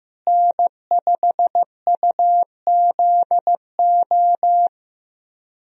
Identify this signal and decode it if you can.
Morse code, heard as a keyed tone, about 15 wpm, 700 Hz. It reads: N5UZO
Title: USA Call Signs - 5 Characters - Repeated 2x - 15wpm